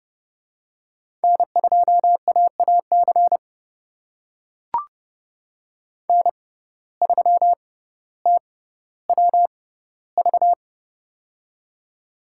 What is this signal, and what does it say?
Morse code, heard as a keyed tone, about 30 wpm, spaced as for 15 wpm, 700 Hz. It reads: N2AAC N3TWV
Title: USA Call Signs - 5 Characters - 6x Character Spacing - 30wpm